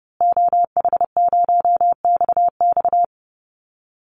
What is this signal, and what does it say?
Morse code, heard as a keyed tone, about 30 wpm, 700 Hz. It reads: OH0XX
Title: Call Signs Repeated 3x - 30wpm